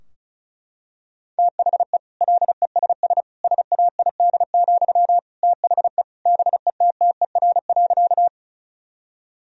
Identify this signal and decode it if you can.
Morse code, heard as a keyed tone, about 36 wpm, 700 Hz. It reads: THE LESS SAID, THE BETTER.
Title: Sentences from Top 1000 Words - Review Part 2 - 35wpm